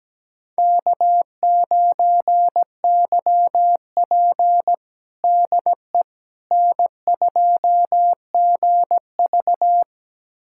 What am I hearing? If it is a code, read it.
Morse code, heard as a keyed tone, about 17 wpm, 700 Hz. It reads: K9YP DE N2GV
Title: USA Callsign DE Callsign - 4 Characters - 17wpm